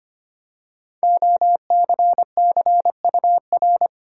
Morse code, heard as a keyed tone, about 25 wpm, 700 Hz. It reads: OCCUR
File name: 5 Letter Words - First 3 Letters Given - 25wpm